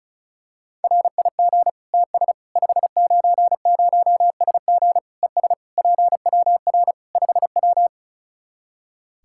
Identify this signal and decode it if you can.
Morse code, heard as a keyed tone, about 36 wpm, 700 Hz. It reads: RIG TS 590SG ES PWR 5W
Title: Practice QSO - Exchange 2 - Info - 35wpm